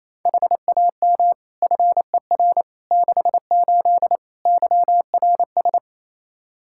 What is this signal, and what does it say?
Morse code, heard as a keyed tone, about 28 wpm, 700 Hz. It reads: HAM FER 68 YRS
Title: K1USN SST QSO Practice 28wpm